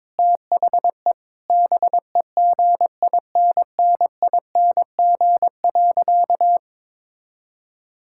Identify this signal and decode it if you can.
Morse code, heard as a keyed tone, about 22 wpm, 700 Hz. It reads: THE BEGINNING.
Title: Sentences from Top 1000 Words - Review Part 5 - 22wpm